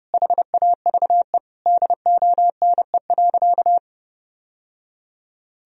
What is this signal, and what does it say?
Morse code, heard as a keyed tone, about 30 wpm, 700 Hz. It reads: HAVE DONE.
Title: Sentences from Top 1000 Words - Review Part 3 - 30wpm